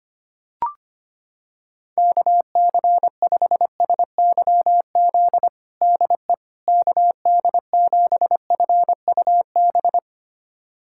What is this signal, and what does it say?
Morse code, heard as a keyed tone, about 25 wpm, 700 Hz. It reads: KC5SYZ DE KD7FUB
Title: USA Callsign DE Callsign - 6 Characters - 25wpm